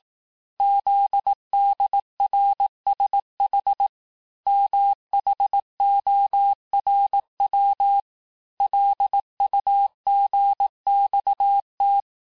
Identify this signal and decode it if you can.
Morse code, heard as a keyed tone, about 18 wpm, 800 Hz. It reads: ZDRSH MHORW LUGXT
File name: cours-du-lundi files / cours-1594--teste-chiffre-vitesse-17mpm